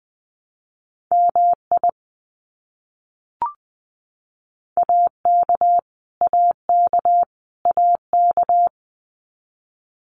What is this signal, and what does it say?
Morse code, heard as a keyed tone, about 19 wpm, 700 Hz. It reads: MI AK AK AK
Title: US States Repeated 3x 20wpm